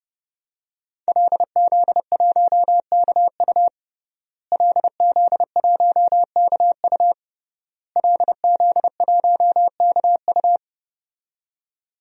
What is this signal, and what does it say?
Morse code, heard as a keyed tone, about 30 wpm, 700 Hz. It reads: LZ1KU LZ1KU LZ1KU
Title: DXCC-100 Call Signs - 5 Characters - Repeated 3x - 30wpm